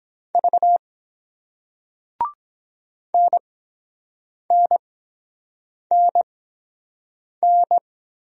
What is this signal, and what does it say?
Morse code, heard as a keyed tone, about 23 wpm, spaced as for 7 wpm, 700 Hz. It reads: V NNNN